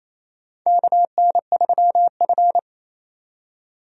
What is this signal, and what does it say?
Morse code, heard as a keyed tone, about 28 wpm, 700 Hz. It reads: KN3F
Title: USA Call Signs - 4 Characters - 28wpm